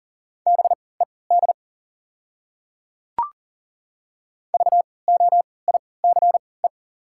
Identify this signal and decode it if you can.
Morse code, heard as a keyed tone, about 40 wpm, spaced as for 23 wpm, 690 Hz. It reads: BED VOICE
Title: 3x Character Spacing - Top 500 Words - 40wpm